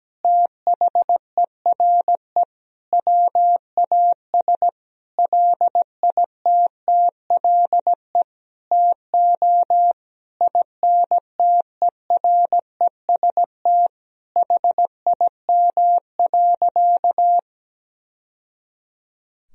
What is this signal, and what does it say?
Morse code, heard as a keyed tone, about 17 wpm, 700 Hz. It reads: THERE WAS LITTLE TO INTEREST HIM.
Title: Sentences from Top 1000 Words - Review Part 2 - 17wpm